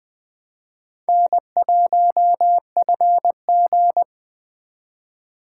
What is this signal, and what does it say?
Morse code, heard as a keyed tone, about 20 wpm, 700 Hz. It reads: N1FG
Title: USA Call Signs - 6x Character Spacing - 20wpm